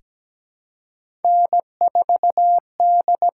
Morse code, heard as a keyed tone, about 17 wpm, 700 Hz. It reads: N4D
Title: USA Call Signs - 6x Character Spacing - 17wpm